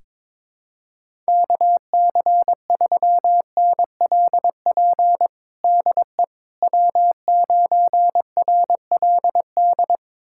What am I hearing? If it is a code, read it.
Morse code, heard as a keyed tone, about 22 wpm, 700 Hz. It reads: KC3NLP DE W9RLD
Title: USA Callsign DE Callsign 22wpm